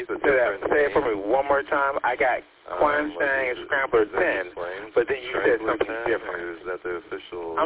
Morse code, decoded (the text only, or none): none